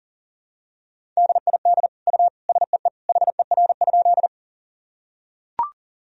Morse code, heard as a keyed tone, about 40 wpm, 700 Hz. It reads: DID U SEE HER?
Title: Sentences from Top 100 Words and Abbreviations 40wpm